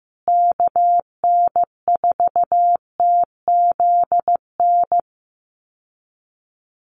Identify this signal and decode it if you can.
Morse code, heard as a keyed tone, about 15 wpm, 700 Hz. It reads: KN4TZN
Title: USA Call Signs Repeated 2x 15wpm